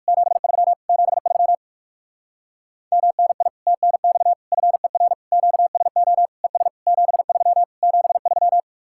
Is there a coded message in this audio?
Yes: Morse code from 0.1 to 8.6 s, heard as a keyed tone, about 58 wpm, 700 Hz. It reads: 73 73 MNI TNX FER QSO ES 73 73